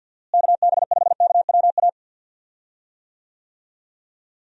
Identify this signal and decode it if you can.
Morse code, heard as a keyed tone, about 50 wpm, 700 Hz. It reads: KB5KWA
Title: USA Call Signs - 6 Characters - Repeated 3x - 50wpm